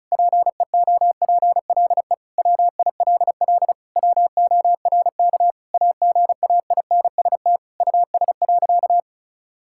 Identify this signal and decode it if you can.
Morse code, heard as a keyed tone, about 36 wpm, 700 Hz. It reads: PEOPLE WILL WORK AGAINST US.
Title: Sentences from Top 1000 Words - Review Part 6 - 35wpm